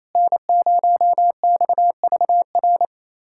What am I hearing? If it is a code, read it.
Morse code, heard as a keyed tone, about 28 wpm, 690 Hz. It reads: N0XVR
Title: USA Callsign DE Callsign - 5 Characters - 28wpm